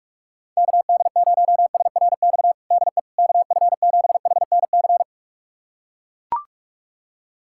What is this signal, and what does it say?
Morse code, heard as a keyed tone, about 41 wpm, 690 Hz. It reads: KD0SRX DE KF7HNC
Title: USA Callsign DE Callsign - 6 Characters - 45wpm